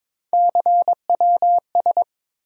Morse code, heard as a keyed tone, about 22 wpm, 700 Hz. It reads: CWS
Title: USA Call Sign Postfixes - 3 Letters 22wpm